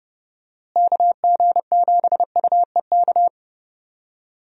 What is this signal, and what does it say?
Morse code, heard as a keyed tone, about 32 wpm, 700 Hz. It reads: KG7UEK
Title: USA Call Signs - 6 Characters - Repeated 3x - 30wpm